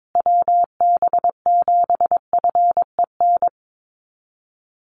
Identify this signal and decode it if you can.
Morse code, heard as a keyed tone, about 22 wpm, 700 Hz. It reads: WB7FEN
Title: USA Call Signs Repeated 3x 22wpm